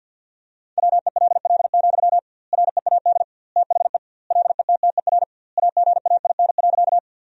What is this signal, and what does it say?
Morse code, heard as a keyed tone, about 49 wpm, 700 Hz. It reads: WELL, READ THE LETTER AGAIN.
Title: 2x Word-Spacing - Sentences from Top 300 Words - 50wpm